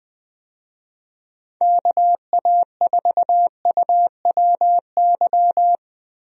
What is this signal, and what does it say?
Morse code, heard as a keyed tone, about 21 wpm, 700 Hz. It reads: KA4UWY
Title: USA Call Signs - 6 Characters - Repeated 3x - 20wpm